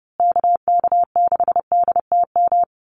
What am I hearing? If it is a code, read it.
Morse code, heard as a keyed tone, about 30 wpm, 700 Hz. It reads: KK6DTM
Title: USA Call Signs Repeated 3x 30wpm